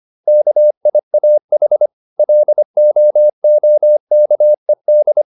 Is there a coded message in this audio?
Yes: Morse code from 0.3 to 5.2 s, heard as a keyed tone, about 25 wpm, 600 Hz. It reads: KIAH LOOKED